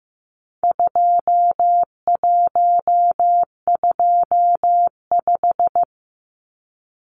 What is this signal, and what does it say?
Morse code, heard as a keyed tone, about 15 wpm, 700 Hz. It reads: 2125
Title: Numbers - 4-Digit 15wpm